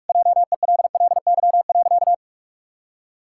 Morse code, heard as a keyed tone, about 44 wpm, 700 Hz. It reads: JELLY.
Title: Pangram Challenge v2 - 45wpm